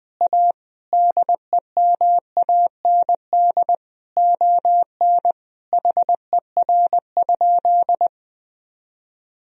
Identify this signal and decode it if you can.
Morse code, heard as a keyed tone, about 21 wpm, 700 Hz. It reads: A DEMAND ON HER?
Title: Sentences from Top 1000 Words - Review Part 4 - 20wpm